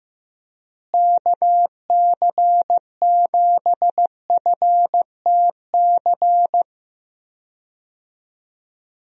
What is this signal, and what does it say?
Morse code, heard as a keyed tone, about 15 wpm, 700 Hz. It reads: KC7FTC
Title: USA Call Signs - 6 Characters - Repeated 3x - 15wpm